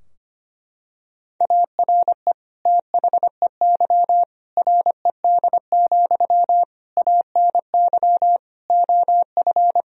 Morse code, heard as a keyed tone, about 25 wpm, 700 Hz. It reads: ARE THEY RED, ANY OF
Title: Sentences from Top 1000 Words - Review Part 2 - 25wpm